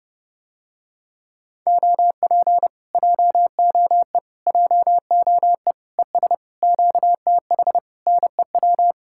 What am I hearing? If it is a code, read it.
Morse code, heard as a keyed tone, about 32 wpm, 700 Hz. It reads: OP JOE JOE ES QTH NEW